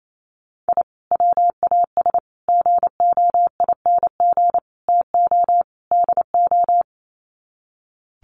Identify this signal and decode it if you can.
Morse code, heard as a keyed tone, about 28 wpm, 700 Hz. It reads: I WAS GOING TO DO
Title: Sets of 5 Words Encore2 28wpm